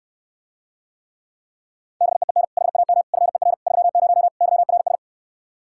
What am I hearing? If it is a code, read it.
Morse code, heard as a keyed tone, about 69 wpm, 700 Hz. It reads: BEEN HAM FER 30 YRS